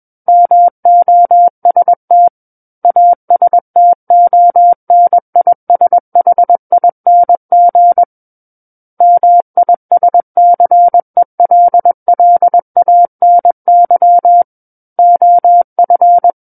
Morse code, heard as a keyed tone, about 21 wpm, 700 Hz. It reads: MOST ASTONISHING MISCELLANY OF
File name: Book One / 12-Chapter 12 - 21 WPM